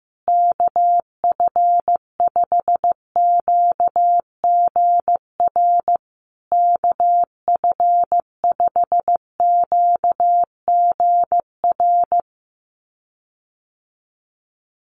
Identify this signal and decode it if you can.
Morse code, heard as a keyed tone, about 15 wpm, 700 Hz. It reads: KF5QGR KF5QGR
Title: USA Call Signs Repeated 3x 15wpm